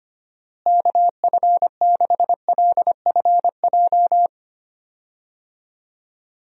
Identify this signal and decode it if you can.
Morse code, heard as a keyed tone, about 25 wpm, 700 Hz. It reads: KF6LFJ
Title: USA Call Signs - 6 Characters - Repeated 2x - 25wpm